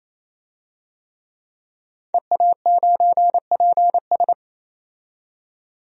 Morse code, heard as a keyed tone, about 28 wpm, 700 Hz. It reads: EA9PS
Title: DXCC-100 Call Signs - 5 Characters - 28wpm